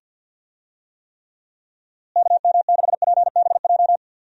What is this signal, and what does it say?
Morse code, heard as a keyed tone, about 52 wpm, 700 Hz. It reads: KM6PBJ
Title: USA Call Signs - 6 Characters - Repeated 2x - 50wpm